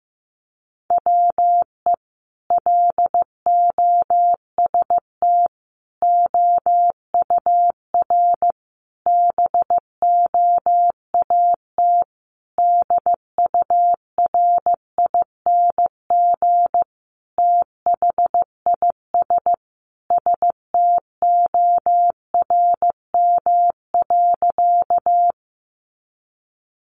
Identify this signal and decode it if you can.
Morse code, heard as a keyed tone, about 14 wpm, 700 Hz. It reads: WE LOST OUR BOAT DURING THIS STORM.
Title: Sentences from Top 1500 Words 15wpm